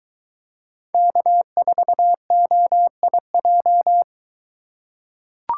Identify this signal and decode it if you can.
Morse code, heard as a keyed tone, about 23 wpm, 700 Hz. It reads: K4OIJ